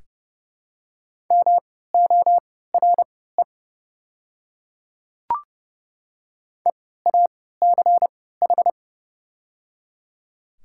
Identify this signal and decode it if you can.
Morse code, heard as a keyed tone, about 30 wpm, spaced as for 17 wpm, 700 Hz. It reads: MORE EACH
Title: 3x Character Spacing - Top 100 Words - 30wpm